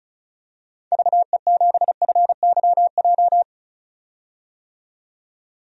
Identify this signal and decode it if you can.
Morse code, heard as a keyed tone, about 36 wpm, 700 Hz. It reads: VE7FYJ
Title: DXCC-100 Call Signs - 6 Characters - 35wpm